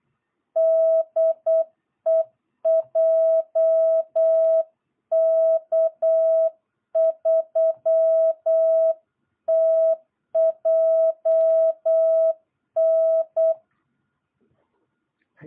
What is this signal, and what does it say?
Morse code, heard as a keyed tone, about 8 wpm, 660 Hz. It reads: DE JK3TJN
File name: jh8cbh / 2022cw